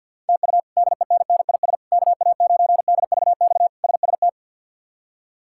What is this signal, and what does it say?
Morse code, heard as a keyed tone, about 47 wpm, 700 Hz. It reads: TU DENNIS KA9DVX SST
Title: K1USN SST QSO Practice - 50wpm